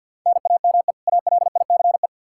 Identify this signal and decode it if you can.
Morse code, heard as a keyed tone, about 51 wpm, 700 Hz. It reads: NAME ALICE
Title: Practice QSO - Exchange 1 - Beginning - 50wpm